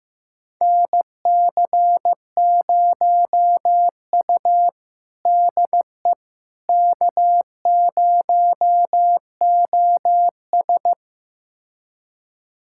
Morse code, heard as a keyed tone, about 15 wpm, 700 Hz. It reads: NC0U DE K0OS
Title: USA Callsign DE Callsign - 4 Characters - 15wpm